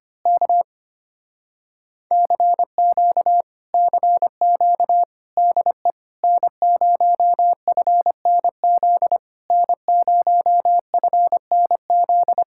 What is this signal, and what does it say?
Morse code, heard as a keyed tone, about 24 wpm, 700 Hz. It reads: K CQ CQ DE N0FNZ N0FNZ
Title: Speed-Racing Common QSO Phrases Encore - 25wpm